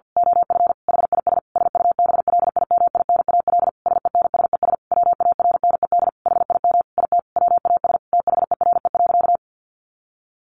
Morse code, heard as a keyed tone, about 50 wpm, 700 Hz. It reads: OF HIS SUBLIMINAL SENSES WARNED HIM IT WAS THERE.
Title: A Sound Of Thunder 50wpm